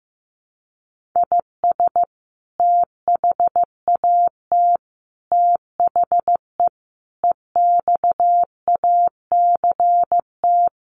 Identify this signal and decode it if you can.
Morse code, heard as a keyed tone, about 15 wpm, 700 Hz. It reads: IS THAT THE EXACT
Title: Sentences from Top 1500 Words 15wpm